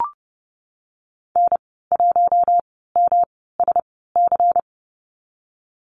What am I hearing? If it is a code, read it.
Morse code, heard as a keyed tone, about 30 wpm, spaced as for 17 wpm, 700 Hz. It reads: N1MSC
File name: Lesson 19g - Letter w - Call Signs 30wpm